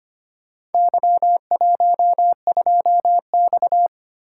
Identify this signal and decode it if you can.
Morse code, heard as a keyed tone, about 25 wpm, 700 Hz. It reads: Y12X
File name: practice / 25wpm / Four Letter-Number Combinations - 25wpm